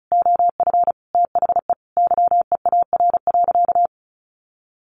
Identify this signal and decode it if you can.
Morse code, heard as a keyed tone, about 34 wpm, 700 Hz. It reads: OF THE YEAR.
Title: Sentences from Top 1400 Words 35wpm